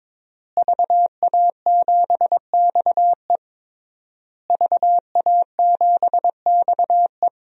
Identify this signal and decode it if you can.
Morse code, heard as a keyed tone, about 22 wpm, 700 Hz. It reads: VA7XE VA7XE
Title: DXCC-100 Call Signs - 5 Characters - Repeated 3x - 22wpm